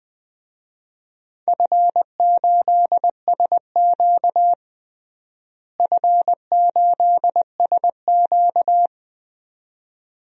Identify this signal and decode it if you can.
Morse code, heard as a keyed tone, about 20 wpm, 700 Hz. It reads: F8SQ F8SQ
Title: DXCC-100 Call Signs - 4 Characters - Repeated 2x - 20wpm